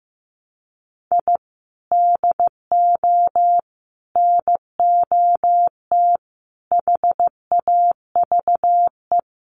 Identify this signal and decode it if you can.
Morse code, heard as a keyed tone, about 15 wpm, 700 Hz. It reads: I DO NOT HAVE
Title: Sets of 5 Words 15wpm